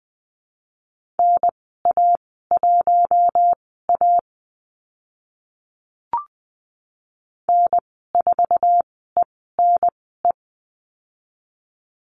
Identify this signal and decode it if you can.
Morse code, heard as a keyed tone, about 20 wpm, spaced as for 14 wpm, 700 Hz. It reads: NA1A N4ENE
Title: 076 Lesson 13g - Letter L - Call Signs 20wpm